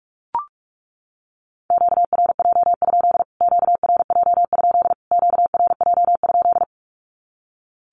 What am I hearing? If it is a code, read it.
Morse code, heard as a keyed tone, about 44 wpm, 700 Hz. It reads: QRJ? QRJ? QRJ?
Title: Q-Codes Repeated 3x 45wpm